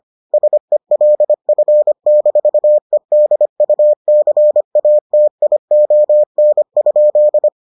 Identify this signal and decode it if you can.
Morse code, heard as a keyed tone, about 25 wpm, 600 Hz. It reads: SELF-EDUCATION?